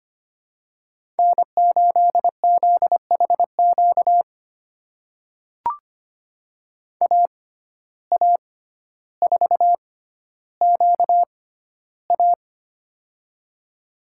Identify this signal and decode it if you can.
Morse code, heard as a keyed tone, about 25 wpm, spaced as for 13 wpm, 700 Hz. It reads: N8ZHQ AA4QA
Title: USA Call Signs - 5 Characters - 6x Character Spacing - 25wpm